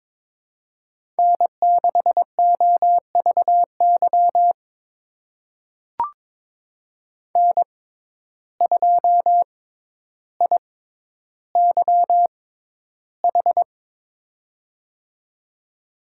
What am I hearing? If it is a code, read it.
Morse code, heard as a keyed tone, about 22 wpm, spaced as for 11 wpm, 700 Hz. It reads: N6OVY N2IYH
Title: USA Call Signs - 5 Characters - 6x Character Spacing - 22wpm